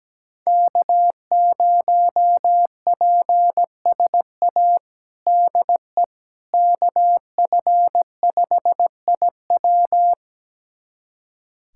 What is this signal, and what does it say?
Morse code, heard as a keyed tone, about 17 wpm, 700 Hz. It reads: K0PSA DE KF5IW
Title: USA Callsign DE Callsign - 5 Characters - 17wpm